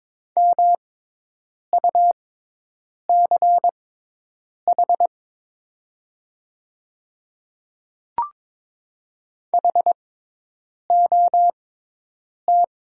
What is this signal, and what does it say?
Morse code, heard as a keyed tone, about 22 wpm, spaced as for 8 wpm, 700 Hz. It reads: MUCH HOT
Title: 6x Character Spacing - Top 200 Words - 22wpm